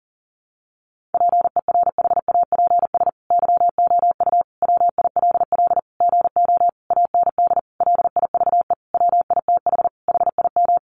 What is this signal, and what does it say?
Morse code, heard as a keyed tone, about 40 wpm, 700 Hz. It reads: PERHAPS YOU WILL GO AND LIVE WITH HIM
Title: Sentences from Top 500 Words 40wpm